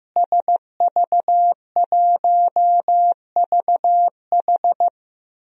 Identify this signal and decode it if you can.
Morse code, heard as a keyed tone, about 15 wpm, 700 Hz. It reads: SV1VH
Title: DXCC-100 Call Signs - 5 Characters - Repeated 2x - 15wpm